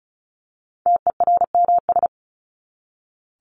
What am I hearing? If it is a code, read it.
Morse code, heard as a keyed tone, about 34 wpm, 700 Hz. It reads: TERMS
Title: ES:S Suffix - Top 100 Words with Base Words 35wpm